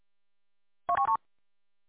There is no code.